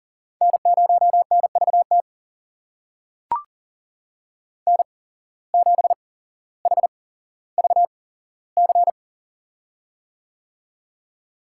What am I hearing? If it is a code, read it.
Morse code, heard as a keyed tone, about 40 wpm, spaced as for 17 wpm, 700 Hz. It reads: N0NVT N7HVC
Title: USA Call Signs - 5 Characters - 8x Character Spacing - 40wpm